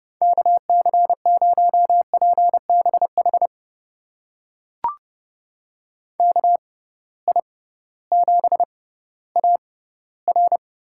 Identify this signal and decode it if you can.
Morse code, heard as a keyed tone, about 30 wpm, spaced as for 16 wpm, 700 Hz. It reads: KC0PBH KI7AR
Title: USA Call Signs - 6 Characters - 6x Character Spacing - 30wpm